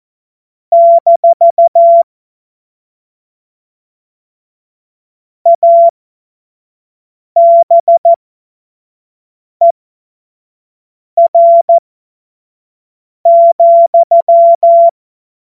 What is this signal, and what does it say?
Morse code, heard as a keyed tone, about 14 wpm, spaced as for 5 wpm, 680 Hz. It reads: - ABER,
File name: Morsekurs / Witze / Witz 16 April 2024 25 F70